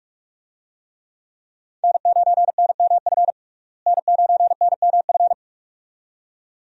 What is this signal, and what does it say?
Morse code, heard as a keyed tone, about 45 wpm, 700 Hz. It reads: N9NMF N9NMF
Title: USA Call Signs - 5 Characters - Repeated 2x - 45wpm